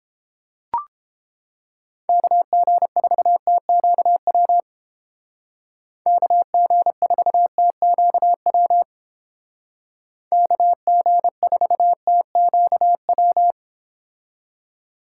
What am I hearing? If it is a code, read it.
Morse code, heard as a keyed tone, about 30 wpm, 700 Hz. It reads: KG4TQW KG4TQW KG4TQW